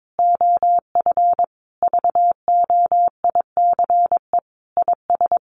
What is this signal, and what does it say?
Morse code, heard as a keyed tone, about 22 wpm, 700 Hz. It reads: OF VOICE IS
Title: Sentences from Top 500 Words 22wpm